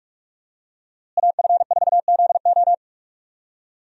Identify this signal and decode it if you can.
Morse code, heard as a keyed tone, about 44 wpm, 700 Hz. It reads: AF4ZO
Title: USA Call Signs - 5 Characters - Repeated 3x - 45wpm